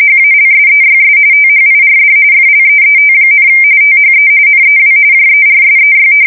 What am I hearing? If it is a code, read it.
RTTY, about 45.45 baud. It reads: HALSEYS THIRD FLEET, THE NEW JERSEY
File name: fsk170 / battleshipnj